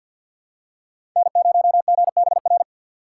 Morse code, heard as a keyed tone, about 48 wpm, 700 Hz. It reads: N0GBR
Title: USA Call Signs - 2x Character Spacing - 50wpm